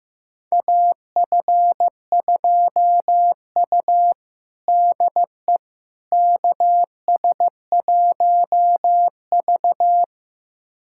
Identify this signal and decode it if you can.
Morse code, heard as a keyed tone, about 16 wpm, 700 Hz. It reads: AF2U DE KS1V
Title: USA Callsign DE Callsign - 4 Characters - 15wpm